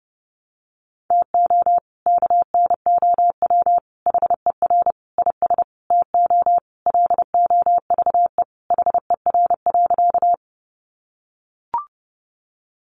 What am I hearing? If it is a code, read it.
Morse code, heard as a keyed tone, about 30 wpm, 700 Hz. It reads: TO KNOW HER IS TO LOVE HER.
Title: Sentences from Top 400 Words 30wpm